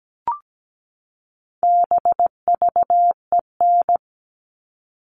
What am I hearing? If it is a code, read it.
Morse code, heard as a keyed tone, about 17 wpm, 700 Hz. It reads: BVEN
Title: All Cut Numbers - 4-Digit 17wpm